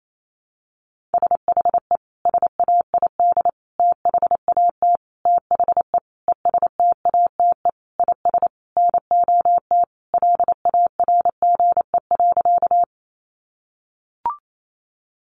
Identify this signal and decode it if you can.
Morse code, heard as a keyed tone, about 28 wpm, 700 Hz. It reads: SHE SAID THAT THE ESTATE IS NOT LARGE.
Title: Sentences from Top 1200 Words 28wpm